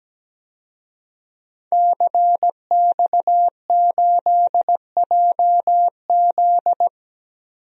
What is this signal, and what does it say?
Morse code, heard as a keyed tone, about 17 wpm, 700 Hz. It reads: CX8JZ
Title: DXCC-100 Call Signs - 5 Characters - 17wpm